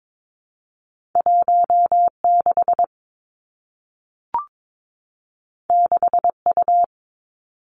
Morse code, heard as a keyed tone, about 22 wpm, 700 Hz. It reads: 1 6 6 U
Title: Two Challenging Letter-Number Combinations 22wpm